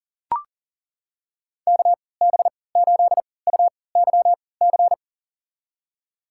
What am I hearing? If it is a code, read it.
Morse code, heard as a keyed tone, about 40 wpm, spaced as for 23 wpm, 700 Hz. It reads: KB8UYC
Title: USA Call Signs - 6 Characters - 3x Character Spacing - 40wpm